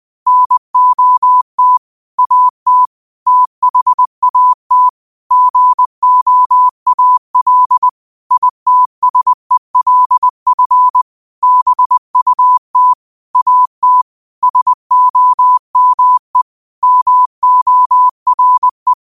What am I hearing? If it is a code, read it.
Morse code, heard as a keyed tone, about 20 wpm, 1000 Hz. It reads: NOT AT THAT GOAL ITSELF BUT AT SOME MORE
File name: qotdfiles / 20wpm